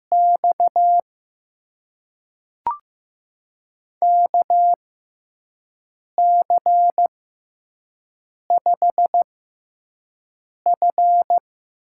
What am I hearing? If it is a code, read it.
Morse code, heard as a keyed tone, about 15 wpm, spaced as for 5 wpm, 700 Hz. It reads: X KC5F
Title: USA Call Signs - 6 Characters - 6x Character Spacing - 15wpm